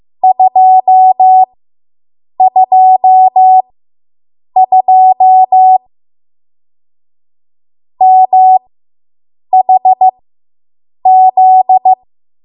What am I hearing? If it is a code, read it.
Morse code, heard as a keyed tone, about 15 wpm, spaced as for 7 wpm, 750 Hz. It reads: 222 MHZ